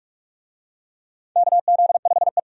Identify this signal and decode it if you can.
Morse code, heard as a keyed tone, about 44 wpm, 700 Hz. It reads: KZ5E
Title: USA Call Signs - 4 Characters - 45wpm